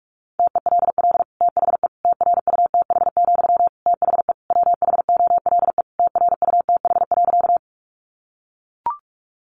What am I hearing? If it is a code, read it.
Morse code, heard as a keyed tone, about 44 wpm, 700 Hz. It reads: TELL THE TRUTH, THE WHOLE TRUTH.
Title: Sentences from Top 800 Words 45wpm